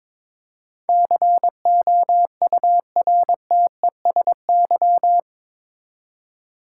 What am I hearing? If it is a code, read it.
Morse code, heard as a keyed tone, about 22 wpm, 700 Hz. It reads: COURTESY
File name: practice / 22wpm / Eight Letter Words - 22wpm